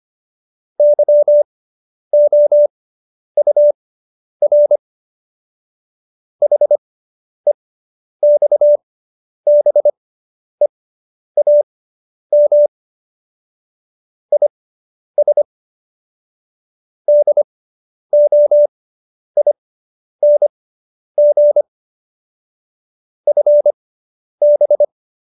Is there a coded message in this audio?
Yes: Morse code from 0.8 to 24.8 s, heard as a keyed tone, about 25 wpm, spaced as for 10 wpm, 590 Hz. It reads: YOUR HEXBEAM IS DOING FB